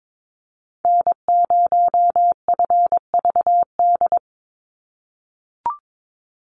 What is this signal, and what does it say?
Morse code, heard as a keyed tone, about 22 wpm, 700 Hz. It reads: N0FVD
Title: USA Call Signs 22wpm